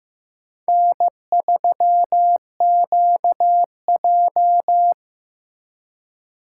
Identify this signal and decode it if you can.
Morse code, heard as a keyed tone, about 18 wpm, 700 Hz. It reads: N3QJ
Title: Speed Racing - USA 1x2 Call Signs 15wpm